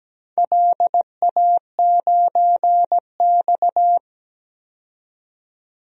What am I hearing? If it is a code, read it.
Morse code, heard as a keyed tone, about 17 wpm, 700 Hz. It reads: LA9X
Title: DXCC-100 Call Signs - 4 Characters - 17wpm